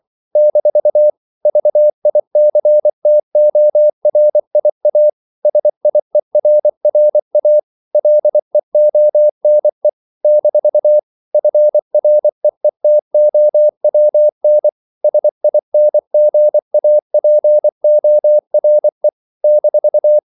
Morse code, heard as a keyed tone, about 24 wpm, 600 Hz. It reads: - VICTORIA SIERRA LEONE - FREETOWN SINGAPORE -
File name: CW / Practice / NationalCapitals24